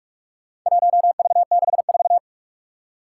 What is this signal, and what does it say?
Morse code, heard as a keyed tone, about 47 wpm, 700 Hz. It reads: 1V64
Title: Four Challenging Letter-Number Combinations 45wpm